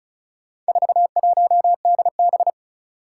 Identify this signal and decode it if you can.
Morse code, heard as a keyed tone, about 36 wpm, 700 Hz. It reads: 41DB
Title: Four Challenging Letter-Number Combinations 35wpm